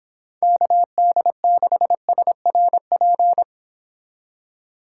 Morse code, heard as a keyed tone, about 26 wpm, 700 Hz. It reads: KD6SRP